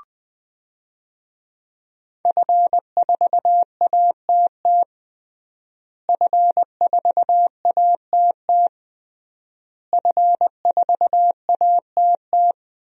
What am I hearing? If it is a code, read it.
Morse code, heard as a keyed tone, about 21 wpm, 700 Hz. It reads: F4ATT F4ATT F4ATT
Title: DXCC-100 Call Signs - 5 Characters - Repeated 3x - 20wpm